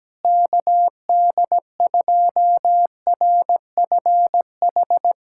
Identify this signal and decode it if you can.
Morse code, heard as a keyed tone, about 17 wpm, 700 Hz. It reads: KD2RFH
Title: USA Callsign DE Callsign 17wpm